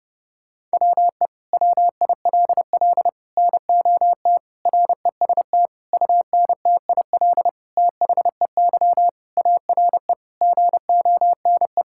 Morse code, heard as a keyed tone, about 30 wpm, 700 Hz. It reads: WE WILL NOT REST UNTIL THEY ARE GONE
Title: Sentences from Top 1600 Words 30wpm